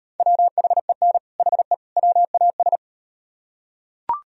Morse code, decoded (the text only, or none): WHEN HE WAS